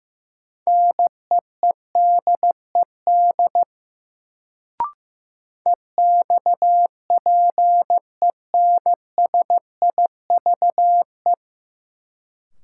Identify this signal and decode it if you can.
Morse code, heard as a keyed tone, about 15 wpm, 700 Hz. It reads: NEEDED EXPENSIVE
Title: Sentences from Top 1000 Words - Review Part 6 - New Word Drill - 15wpm